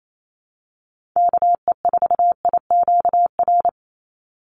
Morse code, heard as a keyed tone, about 28 wpm, 700 Hz. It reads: KE4IQR
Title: USA Call Signs 28wpm